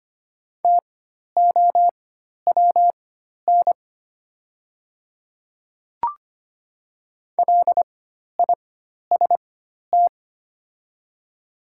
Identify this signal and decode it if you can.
Morse code, heard as a keyed tone, about 25 wpm, spaced as for 12 wpm, 690 Hz. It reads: TOWN LIST
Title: 4x Character Spacing - Top 500 Words - 25wpm